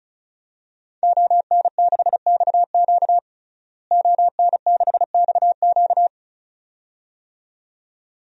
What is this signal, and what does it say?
Morse code, heard as a keyed tone, about 36 wpm, 700 Hz. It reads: ON6XQ ON6XQ
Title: DXCC-100 Call Signs - 5 Characters - Repeated 2x - 35wpm